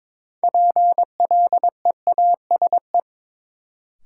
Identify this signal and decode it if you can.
Morse code, heard as a keyed tone, about 22 wpm, 700 Hz. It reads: PLEASE
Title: Sentences from Top 1500 Words - New Word Drill - 22wpm